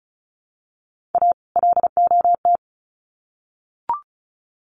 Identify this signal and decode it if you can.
Morse code, heard as a keyed tone, about 34 wpm, 700 Hz. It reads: A LOT
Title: Sets of 2 Words 35wpm